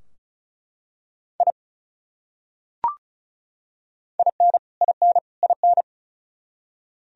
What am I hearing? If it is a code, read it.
Morse code, heard as a keyed tone, about 34 wpm, 700 Hz. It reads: I IN IN IN
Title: Top 100 Words Repeated 35wpm